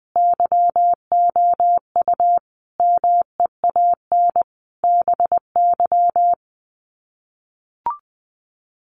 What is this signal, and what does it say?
Morse code, heard as a keyed tone, about 20 wpm, 700 Hz. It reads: YOU MEAN BY
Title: Sets of 5 Words 20wpm